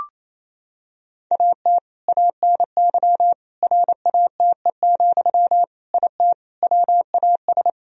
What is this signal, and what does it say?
Morse code, heard as a keyed tone, about 28 wpm, 700 Hz. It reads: AT ANY RATE, IT WAS
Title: Sentences from Top 1000 Words - Review Part 4 - 28wpm